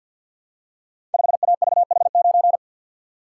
Morse code, heard as a keyed tone, about 47 wpm, 700 Hz. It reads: 5AVH9
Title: Code Groups - 5 Letter:Numbers - 50wpm